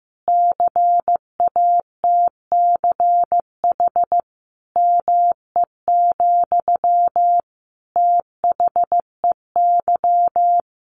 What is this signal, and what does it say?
Morse code, heard as a keyed tone, about 15 wpm, 700 Hz. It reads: CATCH ME, THEY
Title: Sentences from Top 1500 Words 15wpm